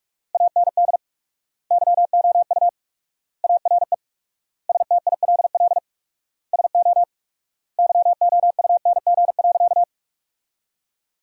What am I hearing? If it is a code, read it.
Morse code, heard as a keyed tone, about 46 wpm, 700 Hz. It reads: AND YOU ARE STILL SO YOUNG.
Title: 4x Word-Spacing - Sentences from Top 400 Words - 45wpm